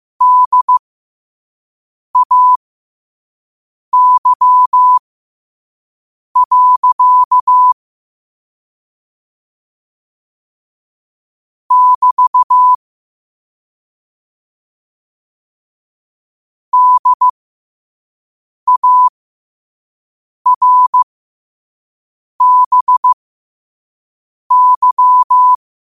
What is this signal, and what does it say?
Morse code, heard as a keyed tone, about 15 wpm, spaced as for 5 wpm, 1000 Hz. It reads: DAY. <BT> DARBY
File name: qotdfiles / 5wpm